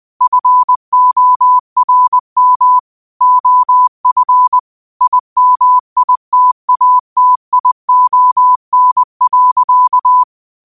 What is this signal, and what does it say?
Morse code, heard as a keyed tone, about 20 wpm, 1000 Hz. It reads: FORM OF IMITATION.
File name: qotdfiles / 20wpm